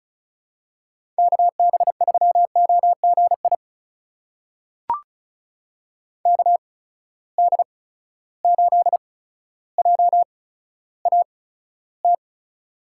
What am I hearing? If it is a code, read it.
Morse code, heard as a keyed tone, about 36 wpm, spaced as for 15 wpm, 700 Hz. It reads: KB3OGI KD8JAT
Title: USA Call Signs - 6 Characters - 8x Character Spacing - 35wpm